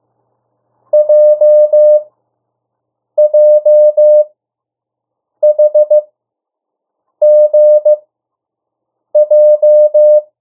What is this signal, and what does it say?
Morse code, heard as a keyed tone, about 15 wpm, spaced as for 6 wpm, 600 Hz. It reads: JJHGJ
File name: morse mp3 / 02 urok 1